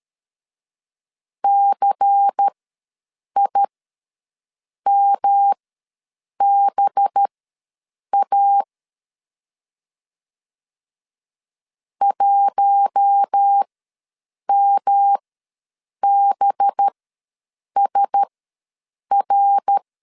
Morse code, heard as a keyed tone, about 13 wpm, spaced as for 7 wpm, 800 Hz. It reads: CIMBA 1MBSR
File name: LEZIONI / 5 LBVC / 5 LBVC ws 12wpm 3159